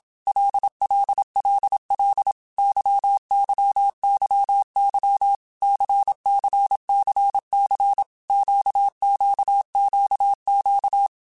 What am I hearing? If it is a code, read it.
Morse code, heard as a keyed tone, about 27 wpm, 800 Hz. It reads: LLLL YYYY CCCC QQQQ
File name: Basic Letter Groups By7Groups / LFCYQ / LFCYQ@25